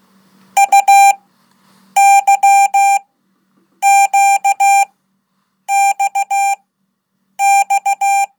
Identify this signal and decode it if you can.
Morse code, heard as a keyed tone, about 16 wpm, spaced as for 8 wpm, 790 Hz. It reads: UYQXX
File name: Code / 5min-1